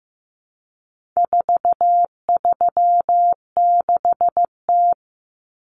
Morse code, heard as a keyed tone, about 15 wpm, 700 Hz. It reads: 436T
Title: Common Cut Numbers - 4-Digit 15wpm